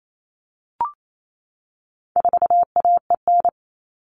Morse code, heard as a keyed tone, about 28 wpm, 700 Hz. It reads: 4AEN
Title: All Cut Numbers - 4-Digit 28wpm